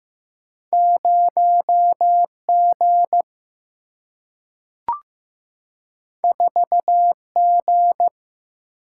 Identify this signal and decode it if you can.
Morse code, heard as a keyed tone, about 15 wpm, 700 Hz. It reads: 0 G 4 G
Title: Single-Letter Single-Number 15wpm